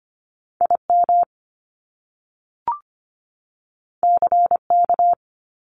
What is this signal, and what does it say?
Morse code, heard as a keyed tone, about 25 wpm, 700 Hz. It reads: IM CK
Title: Top Two Letter Combinations in Words 25wpm